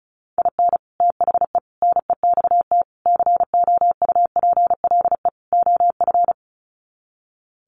Morse code, heard as a keyed tone, about 34 wpm, 700 Hz. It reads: IN THE NEXT COUPLE OF
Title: Sets of 5 Words Encore2 35wpm